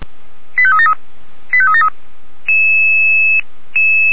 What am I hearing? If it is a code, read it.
Five-tone selective call (ZVEI), heard as quick runs of five tones, tones of about 70 ms. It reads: 85373 85373